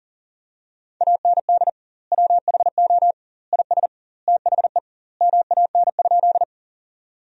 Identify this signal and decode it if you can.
Morse code, heard as a keyed tone, about 40 wpm, 700 Hz. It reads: AND WHO IS THE MAN?
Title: 2x Word-Spacing - Sentences from Top 200 Words - 40wpm